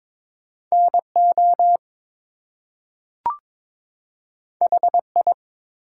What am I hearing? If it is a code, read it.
Morse code, heard as a keyed tone, about 22 wpm, 700 Hz. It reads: NO HI
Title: Two Letter Words - 22wpm